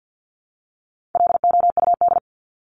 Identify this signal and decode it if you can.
Morse code, heard as a keyed tone, about 50 wpm, 700 Hz. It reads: LOUD
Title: Top 1751-2000 Words 50wpm